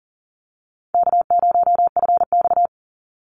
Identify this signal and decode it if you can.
Morse code, heard as a keyed tone, about 40 wpm, 700 Hz. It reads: K0FX
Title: Call Signs - 40wpm